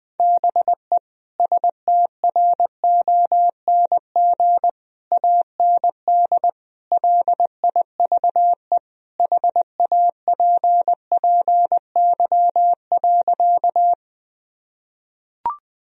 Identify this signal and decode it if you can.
Morse code, heard as a keyed tone, about 20 wpm, 700 Hz. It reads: BE STRONG AND LIVE HAPPY.
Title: Sentences from Top 1000 Words - Review Part 3 - 20wpm